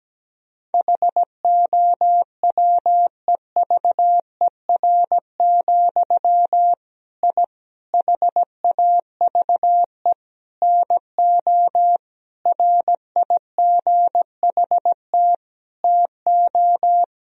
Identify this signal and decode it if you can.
Morse code, heard as a keyed tone, about 17 wpm, 700 Hz. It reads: HOWEVER, I HAVE NO RIGHT TO
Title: Sentences from Top 1000 Words - Review Part 3 - 17wpm